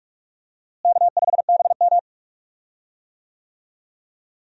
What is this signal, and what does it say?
Morse code, heard as a keyed tone, about 44 wpm, 700 Hz. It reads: K5BM
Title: USA Call Signs - 4 Characters - Repeated 3x - 45wpm